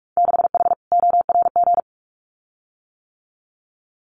Morse code, heard as a keyed tone, about 44 wpm, 700 Hz. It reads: 6H ORG
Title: Winter Field Day Exchanges Encore 45wpm